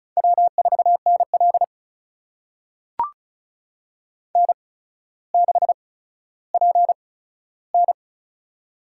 Morse code, heard as a keyed tone, about 34 wpm, spaced as for 15 wpm, 700 Hz. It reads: W4NL N6PN
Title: USA Call Signs - 4 Characters - 8x Character Spacing - 35wpm